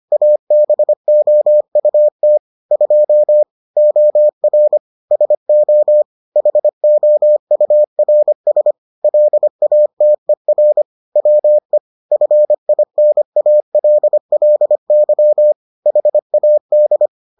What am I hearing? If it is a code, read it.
Morse code, heard as a keyed tone, about 25 wpm, 600 Hz. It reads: ABOUT 2 OR SO HOURS LATER WE FINALLY HAD